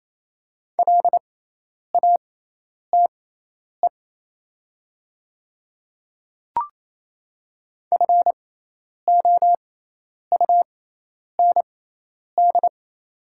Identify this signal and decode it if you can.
Morse code, heard as a keyed tone, about 28 wpm, spaced as for 10 wpm, 700 Hz. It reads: LATE FOUND
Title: 6x Character Spacing - Top 500 Words - 28wpm